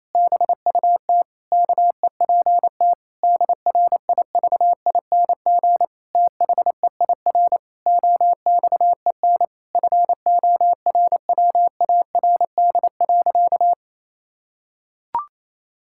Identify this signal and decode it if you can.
Morse code, heard as a keyed tone, about 28 wpm, 700 Hz. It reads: BUT KEPT DRIVING THEIR OXEN FORWARD.
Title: Pangram Challenge v2 - 28wpm